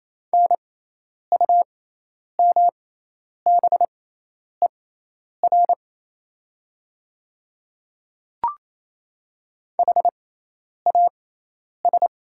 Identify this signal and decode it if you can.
Morse code, heard as a keyed tone, about 28 wpm, spaced as for 10 wpm, 700 Hz. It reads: NUMBER HAS
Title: 6x Character Spacing - Top 100 Words - 28wpm